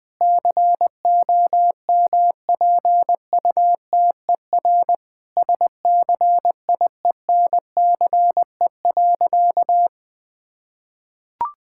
Morse code, heard as a keyed tone, about 20 wpm, 700 Hz. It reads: COMPUTER SCIENCE.
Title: Sentences from Top 1000 Words - Review Part 6 - 20wpm